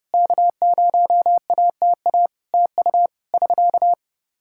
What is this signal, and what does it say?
Morse code, heard as a keyed tone, about 30 wpm, 700 Hz. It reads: K0ATA TU <SK>
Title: Practice QSO - Exchange 4 - Ending - 30wpm